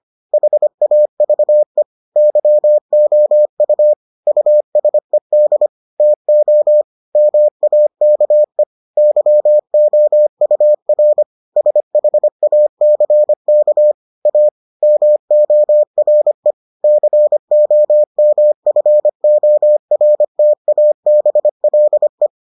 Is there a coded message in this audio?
Yes: Morse code from 0.3 to 22.3 s, heard as a keyed tone, about 25 wpm, 600 Hz. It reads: HAVE YOU USED TO MAKE YOUR SHACK A MORE COMFORTABLE